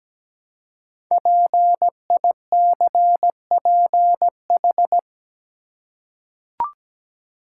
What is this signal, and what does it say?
Morse code, heard as a keyed tone, about 17 wpm, 700 Hz. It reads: PICPH
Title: Code Groups - 5 Letters - 17wpm